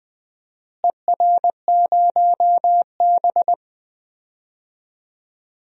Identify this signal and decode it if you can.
Morse code, heard as a keyed tone, about 21 wpm, 700 Hz. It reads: ER0B
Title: DXCC-100 Call Signs - 4 Characters - Repeated 3x - 20wpm